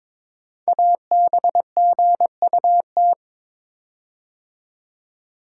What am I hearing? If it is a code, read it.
Morse code, heard as a keyed tone, about 22 wpm, 700 Hz. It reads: ABGUT
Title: Code Groups - 5 Letters - 22wpm